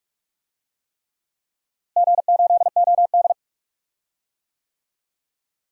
Morse code, heard as a keyed tone, about 43 wpm, 700 Hz. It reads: G8OD
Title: DXCC-100 Call Signs - 4 Characters - Repeated 3x - 45wpm